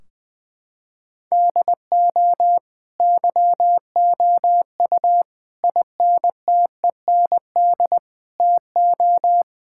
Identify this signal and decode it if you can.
Morse code, heard as a keyed tone, about 20 wpm, 700 Hz. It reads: DO YOU INTEND TO
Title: Sentences from Top 1600 Words 20wpm